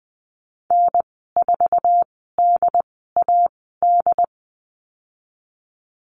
Morse code, heard as a keyed tone, about 20 wpm, spaced as for 15 wpm, 700 Hz. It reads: N4DAD
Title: 076 Lesson 13g - Letter L - Call Signs 20wpm